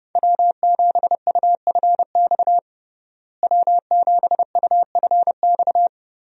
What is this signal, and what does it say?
Morse code, heard as a keyed tone, about 32 wpm, 700 Hz. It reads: W7UFX W7UFX
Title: USA Call Signs - 5 Characters - Repeated 2x - 30wpm